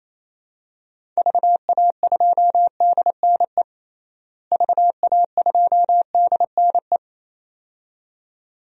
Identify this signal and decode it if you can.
Morse code, heard as a keyed tone, about 28 wpm, 700 Hz. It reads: VA2DNE VA2DNE
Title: DXCC-100 Call Signs - 6 Characters - Repeated 2x - 28wpm